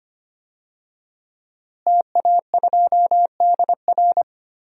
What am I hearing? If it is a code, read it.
Morse code, heard as a keyed tone, about 25 wpm, 700 Hz. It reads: TA2DR
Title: DXCC-100 Call Signs - 5 Characters - Repeated 2x - 25wpm